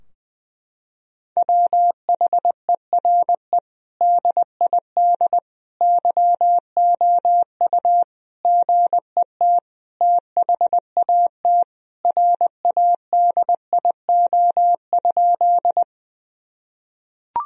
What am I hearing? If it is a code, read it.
Morse code, heard as a keyed tone, about 20 wpm, 700 Hz. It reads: WHERE DID YOU GET THAT RADIO?
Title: Sentences from Top 1000 Words - Review Part 3 - 20wpm